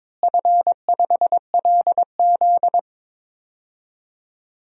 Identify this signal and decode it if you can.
Morse code, heard as a keyed tone, about 22 wpm, 700 Hz. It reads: F5LZ
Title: DXCC-100 Call Signs - 4 Characters - Repeated 2x - 22wpm